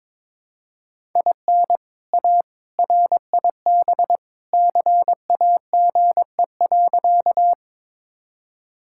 Morse code, heard as a keyed tone, about 22 wpm, 700 Hz. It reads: IN A RIB CAGE.
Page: morse-code-ninja-practice-22wpm